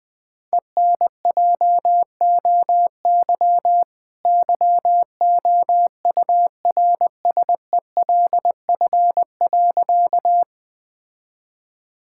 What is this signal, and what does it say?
Morse code, heard as a keyed tone, about 20 wpm, 700 Hz. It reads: ENJOY YOURSELF.
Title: Sentences from Top 1000 Words - Review Part 5 - 20wpm